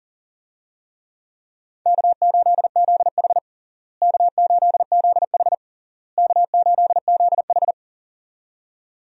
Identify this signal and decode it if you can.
Morse code, heard as a keyed tone, about 40 wpm, 700 Hz. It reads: K8ZH K8ZH K8ZH
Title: USA Call Signs - 4 Characters - Repeated 3x - 40wpm